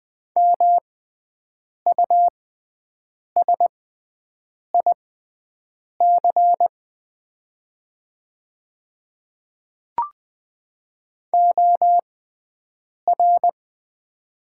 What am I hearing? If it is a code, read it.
Morse code, heard as a keyed tone, about 20 wpm, spaced as for 7 wpm, 700 Hz. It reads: MUSIC OR
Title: 6x Character Spacing - Top 500 Words - 20wpm